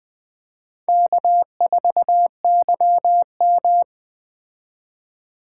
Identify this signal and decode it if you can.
Morse code, heard as a keyed tone, about 20 wpm, 700 Hz. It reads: K4YM